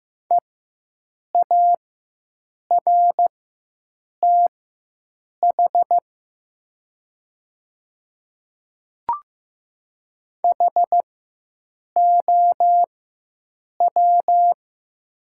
Morse code, heard as a keyed tone, about 15 wpm, spaced as for 7 wpm, 700 Hz. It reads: EARTH HOW
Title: 4x Character Spacing - Top 200 Words - 15wpm